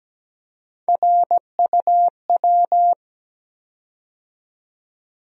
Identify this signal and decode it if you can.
Morse code, heard as a keyed tone, about 17 wpm, 700 Hz. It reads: RUW
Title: USA Call Sign Postfixes - 3 Letters 17wpm